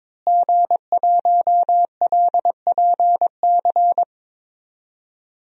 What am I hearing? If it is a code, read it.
Morse code, heard as a keyed tone, about 21 wpm, 700 Hz. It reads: G1LPC
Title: DXCC-100 Call Signs - 5 Characters - Repeated 2x - 22wpm